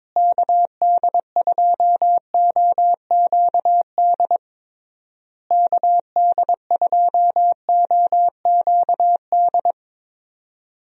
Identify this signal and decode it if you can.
Morse code, heard as a keyed tone, about 22 wpm, 700 Hz. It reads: KD2OQD KD2OQD
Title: USA Call Signs - 6 Characters - Repeated 3x - 22wpm